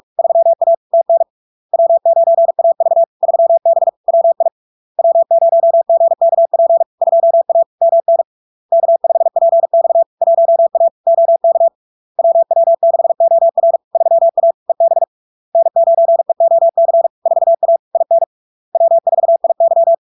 Morse code, heard as a keyed tone, about 46 wpm, 670 Hz. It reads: 3A TN W9AV 3B WI W0GKP 2A MN K5PX 1A OK WW6OR 3A EB N9EOC 4A IN W4IY
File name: LARK-FD-2022-Log-in-CW-45wpm-670-Hz